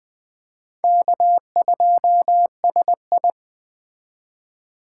Morse code, heard as a keyed tone, about 21 wpm, 700 Hz. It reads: K2SI